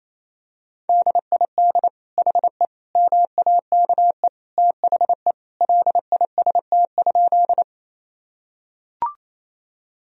Morse code, heard as a keyed tone, about 28 wpm, 700 Hz. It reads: DID HE MAKE THE LIST?
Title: Sentences from Top 1000 Words - Review Part 2 - 28wpm